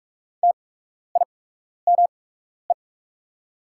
Morse code, heard as a keyed tone, about 44 wpm, spaced as for 12 wpm, 700 Hz. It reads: TIME
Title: 8x Character Spacing - Top 100 Words - 45wpm